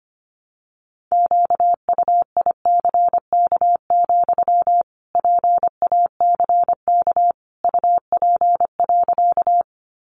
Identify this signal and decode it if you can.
Morse code, heard as a keyed tone, about 25 wpm, 700 Hz. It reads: QUICK, PACK UP.
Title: Sentences from Top 1300 Words 25wpm